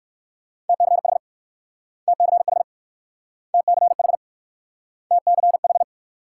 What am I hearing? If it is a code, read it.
Morse code, heard as a keyed tone, about 56 wpm, 700 Hz. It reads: TCH TCH TCH TCH
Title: Speed-Racing Multi-letter Phonemes - 45wpm